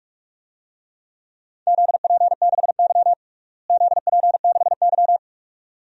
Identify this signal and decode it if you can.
Morse code, heard as a keyed tone, about 44 wpm, 700 Hz. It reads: ZP6Y ZP6Y
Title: DXCC-100 Call Signs - 4 Characters - Repeated 2x - 45wpm